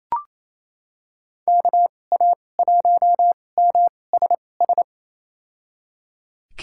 Morse code, heard as a keyed tone, about 28 wpm, spaced as for 20 wpm, 700 Hz. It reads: KA1MSS
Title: USA Call Signs - 6 Characters - 2x Character Spacing - 28wpm